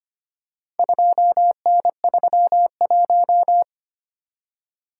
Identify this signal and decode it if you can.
Morse code, heard as a keyed tone, about 25 wpm, 700 Hz. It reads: 2N31
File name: practice / 25wpm / Four Letter-Number Combinations - 25wpm